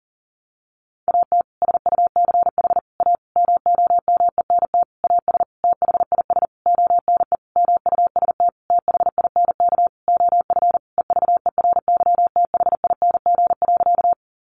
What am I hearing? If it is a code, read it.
Morse code, heard as a keyed tone, about 37 wpm, 700 Hz. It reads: AT SUCH A MOMENT AS THIS ONE MUST THINK OF EVERYTHING.
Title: Sentences from Top 1000 Words 40wpm